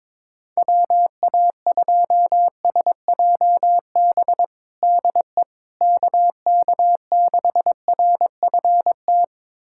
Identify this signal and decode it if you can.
Morse code, heard as a keyed tone, about 22 wpm, 700 Hz. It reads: WA2SJB DE KK6RFT
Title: USA Callsign DE Callsign - 6 Characters - 22wpm